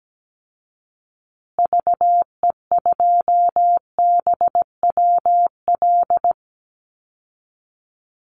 Morse code, heard as a keyed tone, about 17 wpm, 700 Hz. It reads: VE2BWL
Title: Call Signs Repeated 3x - 17wpm